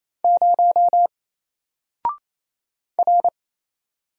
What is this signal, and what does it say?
Morse code, heard as a keyed tone, about 27 wpm, 700 Hz. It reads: 0 R
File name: Single-Letter-Number-28wpm